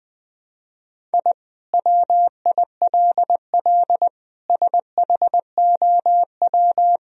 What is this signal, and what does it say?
Morse code, heard as a keyed tone, about 20 wpm, 700 Hz. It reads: I WILL SHOW
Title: Sentences from Top 1000 Words - Review Part 1 - 20wpm